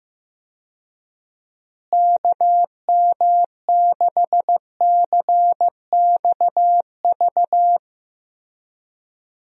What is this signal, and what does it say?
Morse code, heard as a keyed tone, about 15 wpm, 700 Hz. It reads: KM6CXV
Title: USA Call Signs - 6 Characters - Repeated 3x - 15wpm